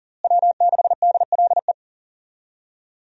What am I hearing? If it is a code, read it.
Morse code, heard as a keyed tone, about 40 wpm, 700 Hz. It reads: W6DLE
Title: USA Call Signs - 5 Characters - Repeated 3x - 40wpm